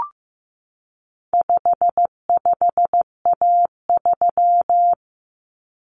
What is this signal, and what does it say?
Morse code, heard as a keyed tone, about 14 wpm, 700 Hz. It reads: 55A3
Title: Common Cut Numbers - 4-Digit 15wpm